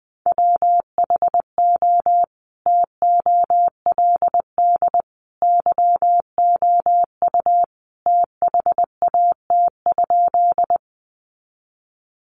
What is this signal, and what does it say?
Morse code, heard as a keyed tone, about 20 wpm, 700 Hz. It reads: WHO TOLD YOU THAT?
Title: Sentences from Top 400 Words 20wpm